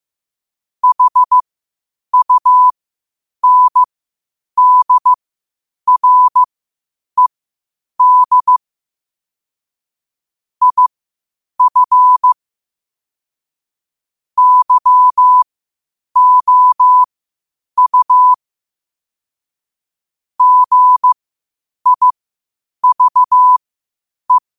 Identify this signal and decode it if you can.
Morse code, heard as a keyed tone, about 15 wpm, spaced as for 8 wpm, 1000 Hz. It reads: HUNDRED IF YOU GIVE
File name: qotdfiles / 8wpm